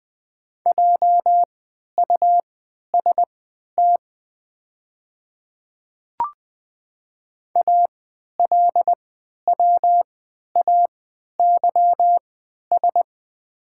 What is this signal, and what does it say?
Morse code, heard as a keyed tone, about 20 wpm, spaced as for 11 wpm, 700 Hz. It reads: JUST ALWAYS
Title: 3x Character Spacing - Top 300 Words - 20wpm